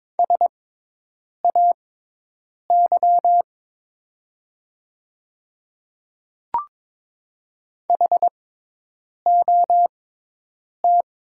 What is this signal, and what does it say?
Morse code, heard as a keyed tone, about 22 wpm, spaced as for 8 wpm, 700 Hz. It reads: SAY HOT
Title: 6x Character Spacing - Top 200 Words - 22wpm